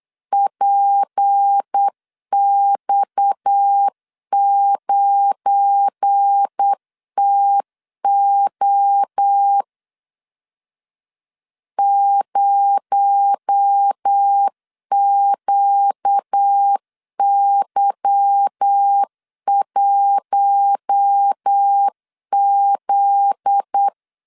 Morse code, heard as a keyed tone, about 8 wpm, 800 Hz. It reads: PX9TO 0QY1Z
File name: LEZIONI / 8 YJ45 / 8 Y 8wpm 3245